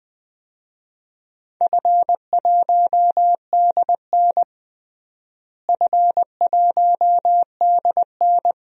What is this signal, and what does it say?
Morse code, heard as a keyed tone, about 20 wpm, 700 Hz. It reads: F1DN F1DN
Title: DXCC-100 Call Signs - 4 Characters - Repeated 3x - 20wpm